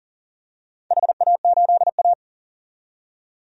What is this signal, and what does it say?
Morse code, heard as a keyed tone, about 40 wpm, 700 Hz. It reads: HA8A
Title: DXCC-100 Call Signs - 4 Characters - 40wpm